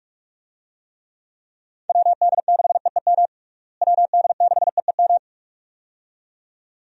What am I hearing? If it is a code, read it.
Morse code, heard as a keyed tone, about 46 wpm, 700 Hz. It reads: WD6EEM WD6EEM
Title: USA Call Signs - 6 Characters - Repeated 2x - 45wpm